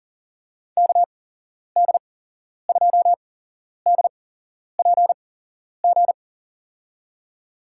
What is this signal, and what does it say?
Morse code, heard as a keyed tone, about 39 wpm, spaced as for 11 wpm, 700 Hz. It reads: KD2DPG
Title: USA Call Signs - 8x Character Spacing - 40wpm